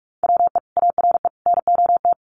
Morse code, heard as a keyed tone, about 44 wpm, 700 Hz. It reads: WE ARE NOT
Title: Sentences from Top 1500 Words 45wpm